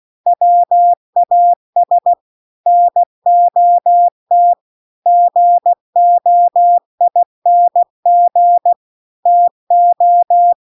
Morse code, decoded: WAS NOT GOING TO

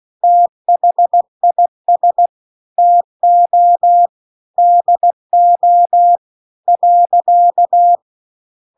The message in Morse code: THIS TO DO .

